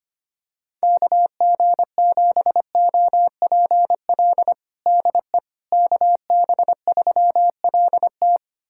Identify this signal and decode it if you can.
Morse code, heard as a keyed tone, about 25 wpm, 700 Hz. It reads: KG7OPL DE KB3LT
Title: USA Callsign DE Callsign 25wpm